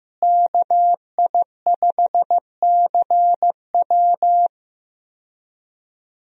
Morse code, heard as a keyed tone, about 15 wpm, 700 Hz. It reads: KI5CW
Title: USA Call Signs - 5 Characters - Repeated 2x - 15wpm